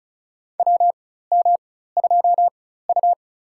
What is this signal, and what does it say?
Morse code, heard as a keyed tone, about 36 wpm, spaced as for 16 wpm, 700 Hz. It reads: WM2U
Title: USA Call Signs - 4x Character Spacing - 35wpm